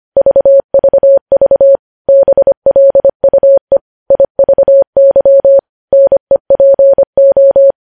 Morse code, heard as a keyed tone, about 25 wpm, 550 Hz. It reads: VVV BLUE IVY NEPO
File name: podcast / News headlines-1x-25x25